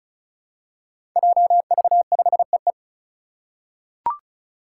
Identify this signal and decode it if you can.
Morse code, heard as a keyed tone, about 34 wpm, 700 Hz. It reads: JV5EE
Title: Five Challenging Letter-Number Combinations 35wpm